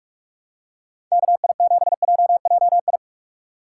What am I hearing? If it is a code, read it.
Morse code, heard as a keyed tone, about 46 wpm, 700 Hz. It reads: KI7JJI
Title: USA Call Signs - 2x Character Spacing - 45wpm